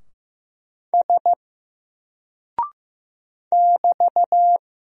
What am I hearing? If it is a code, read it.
Morse code, heard as a keyed tone, about 15 wpm, 700 Hz. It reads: S <BT>
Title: Single Character (Letter:Number:Punctuation:Prosign) 15wpm